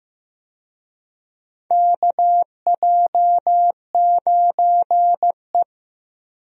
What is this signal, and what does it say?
Morse code, heard as a keyed tone, about 15 wpm, 700 Hz. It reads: KJ9E
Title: USA Call Signs - 4 Characters - Repeated 3x - 15wpm